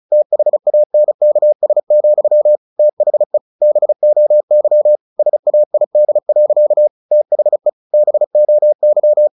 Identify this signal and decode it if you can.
Morse code, heard as a keyed tone, about 34 wpm, 600 Hz. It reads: THANKS, THE BOY SAID. THE BOY
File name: the old man and the sea 2 35wpm